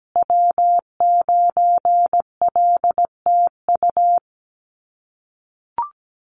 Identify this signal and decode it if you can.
Morse code, heard as a keyed tone, about 17 wpm, 700 Hz. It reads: W9LTU
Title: USA Call Signs Repeated 3x 17wpm